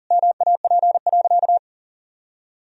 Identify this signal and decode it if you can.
Morse code, heard as a keyed tone, about 40 wpm, 700 Hz. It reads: MAP.
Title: Sentences from Top 1000 Words - Review Part 2 - 40wpm